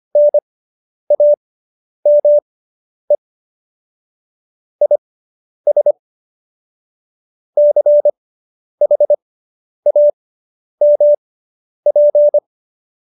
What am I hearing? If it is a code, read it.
Morse code, heard as a keyed tone, about 25 wpm, spaced as for 10 wpm, 590 Hz. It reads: NAME IS CHAMP